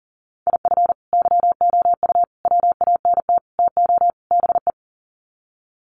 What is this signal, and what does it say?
Morse code, heard as a keyed tone, about 40 wpm, 700 Hz. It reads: IF YOU WANT TO BE
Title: Sets of 5 Words Encore 40wpm